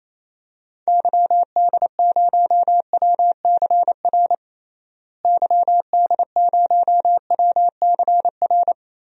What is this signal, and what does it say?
Morse code, heard as a keyed tone, about 28 wpm, 700 Hz. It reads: YD0WCR YD0WCR
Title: DXCC-100 Call Signs - 6 Characters - Repeated 3x - 28wpm